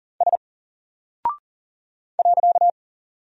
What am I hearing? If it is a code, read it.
Morse code, heard as a keyed tone, about 40 wpm, 700 Hz. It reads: S .
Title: Single Character (Letter:Number:Punctuation:Prosign) 40wpm